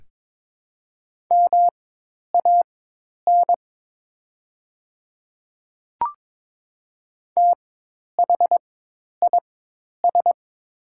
Morse code, heard as a keyed tone, about 22 wpm, spaced as for 10 wpm, 700 Hz. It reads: MAN THIS
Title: 4x Character Spacing - Top 300 Words - 22wpm